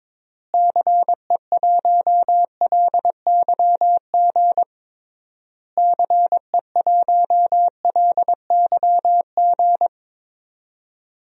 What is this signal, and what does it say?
Morse code, heard as a keyed tone, about 22 wpm, 700 Hz. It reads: CE1LYG CE1LYG
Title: DXCC-100 Call Signs - 6 Characters - Repeated 3x - 22wpm